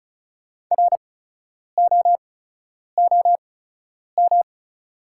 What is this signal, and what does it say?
Morse code, heard as a keyed tone, about 34 wpm, spaced as for 10 wpm, 700 Hz. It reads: ROOM
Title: 8x Character Spacing - Top 500 Words - 35wpm